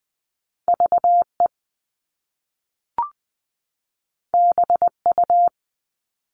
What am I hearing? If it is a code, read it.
Morse code, heard as a keyed tone, about 20 wpm, 700 Hz. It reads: VE BU
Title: All Cut Numbers - 2-Digit 20wpm